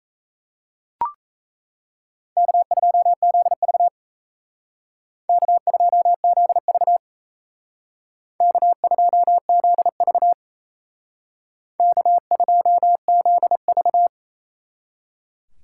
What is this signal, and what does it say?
Morse code, heard as a keyed tone, about 35 wpm, 700 Hz. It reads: K2ZV K2ZV K2ZV K2ZV